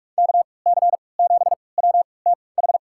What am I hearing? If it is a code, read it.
Morse code, heard as a keyed tone, about 46 wpm, spaced as for 26 wpm, 700 Hz. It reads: KC7WTH
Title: USA Call Signs - 6 Characters - 3x Character Spacing - 45wpm